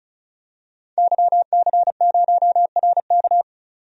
Morse code, heard as a keyed tone, about 34 wpm, 700 Hz. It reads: YC0RK
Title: DXCC-100 Call Signs - 5 Characters - 35wpm